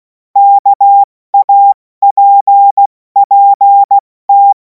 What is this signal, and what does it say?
Morse code, heard as a keyed tone, about 16 wpm, 800 Hz. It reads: KAPPT
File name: Morsekurs / Witze / Witz 17. Februar 2026 70 F80